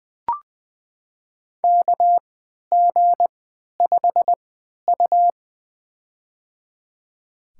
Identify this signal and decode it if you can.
Morse code, heard as a keyed tone, about 20 wpm, spaced as for 11 wpm, 700 Hz. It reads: KG5U
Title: USA Call Signs - 3x Character Spacing - 20wpm